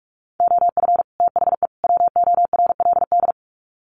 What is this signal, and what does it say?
Morse code, heard as a keyed tone, about 44 wpm, 700 Hz. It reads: OF THE WORLD
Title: Sets of 3 Words 45wpm